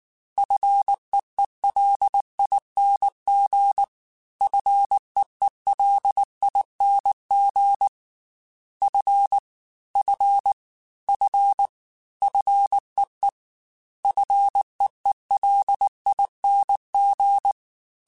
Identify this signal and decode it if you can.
Morse code, heard as a keyed tone, about 19 wpm, 800 Hz. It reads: FEELING FEELING F F F FEE FEELING
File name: Fs / Single F mp3s / Single F Prefixes 3x2 Twice @18